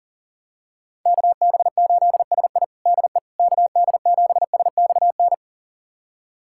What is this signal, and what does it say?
Morse code, heard as a keyed tone, about 40 wpm, 700 Hz. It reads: KB8SI DE KD7SXN
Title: USA Callsign DE Callsign 40wpm